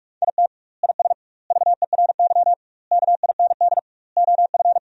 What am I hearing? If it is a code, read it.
Morse code, heard as a keyed tone, about 46 wpm, 700 Hz. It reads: IT IS VERY KIND OF
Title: 2x Word-Spacing - Sentences from Top 200 Words - 45wpm